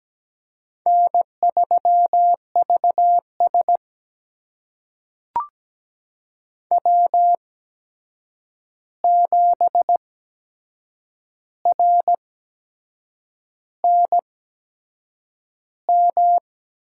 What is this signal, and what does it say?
Morse code, heard as a keyed tone, about 17 wpm, spaced as for 7 wpm, 700 Hz. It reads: N3VS W7RNM
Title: USA Call Signs - 8x Character Spacing - 17wpm